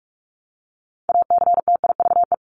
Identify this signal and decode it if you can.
Morse code, heard as a keyed tone, about 44 wpm, 700 Hz. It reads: ACTIVE
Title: Top 751-1000 Words 45wpm